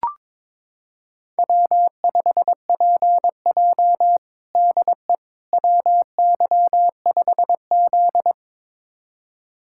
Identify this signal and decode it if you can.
Morse code, heard as a keyed tone, about 22 wpm, 700 Hz. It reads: W5PJ DE WY5Z
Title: USA Callsign DE Callsign - 4 Characters - 22wpm